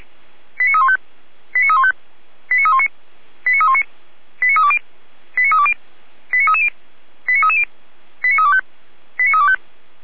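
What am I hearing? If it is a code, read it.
Five-tone selective call (ZVEI), heard as quick runs of five tones, tones of about 70 ms. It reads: 89315 89315 89319 89319 89320 89320 89330 89330 89325 89325
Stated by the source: ZVEI five-tone call received by SDR, from Feuerwehr Orlamünde (fire brigade) archive.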